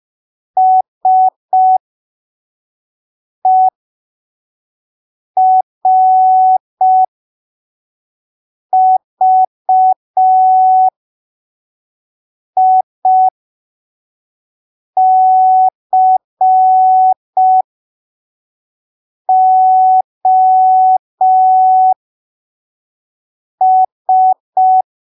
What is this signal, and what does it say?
Morse code, heard as a keyed tone, about 5 wpm, spaced as for 3 wpm, 750 Hz. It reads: SERVICOS
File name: arquivos / texto / texto 12 05ppm